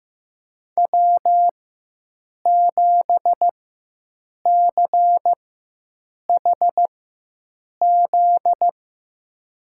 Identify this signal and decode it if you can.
Morse code, heard as a keyed tone, about 15 wpm, spaced as for 7 wpm, 700 Hz. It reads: W7CHZ
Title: USA Call Signs - 4x Character Spacing - 15wpm